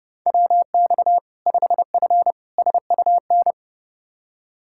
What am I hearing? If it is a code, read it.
Morse code, heard as a keyed tone, about 30 wpm, 700 Hz. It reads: WX 5F SUN
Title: Practice QSO - Exchange 2 - Info - 30wpm